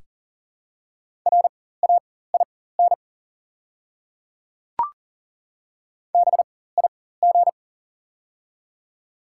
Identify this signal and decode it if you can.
Morse code, heard as a keyed tone, about 40 wpm, spaced as for 19 wpm, 700 Hz. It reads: RAIN BIG
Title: 4x Character Spacing - Top 500 Words - 40wpm